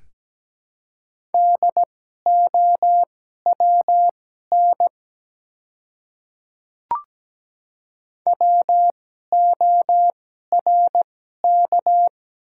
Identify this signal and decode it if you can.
Morse code, heard as a keyed tone, about 17 wpm, spaced as for 12 wpm, 700 Hz. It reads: DOWN WORK
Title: 2x Character Spacing - Top 200 Words - 17wpm